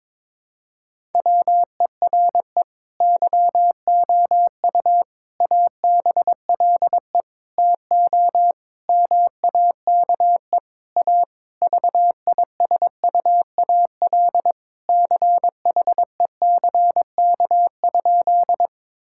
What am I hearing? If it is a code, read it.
Morse code, heard as a keyed tone, about 22 wpm, 700 Hz. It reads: WERE YOU ABLE TO MAKE A VISUAL CHECK?
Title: Sentences from Top 1000 Words - Review Part 6 - 22wpm